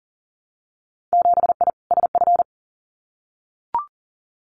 Morse code, heard as a keyed tone, about 40 wpm, 700 Hz. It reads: 7I SF
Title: Winter Field Day Exchanges Encore 40wpm